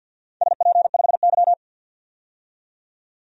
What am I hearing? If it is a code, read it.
Morse code, heard as a keyed tone, about 53 wpm, 700 Hz. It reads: SP5Y
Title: DXCC-100 Call Signs - 4 Characters - 50wpm